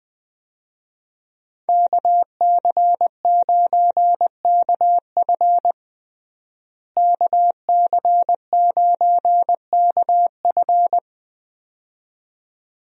Morse code, heard as a keyed tone, about 20 wpm, 700 Hz. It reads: KC9KF KC9KF
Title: USA Call Signs - 5 Characters - Repeated 2x - 20wpm